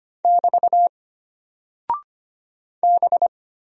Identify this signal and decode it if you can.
Morse code, heard as a keyed tone, about 25 wpm, 700 Hz. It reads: <BT> B
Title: Single Character (Letter:Number:Punctuation:Prosign) Alt 25wpm